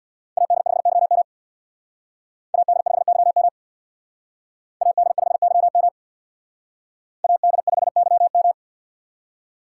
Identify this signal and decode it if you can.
Morse code, heard as a keyed tone, about 62 wpm, 700 Hz. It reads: AD5YM AD5YM AD5YM AD5YM
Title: Speed Racing - USA Call Signs 50wpm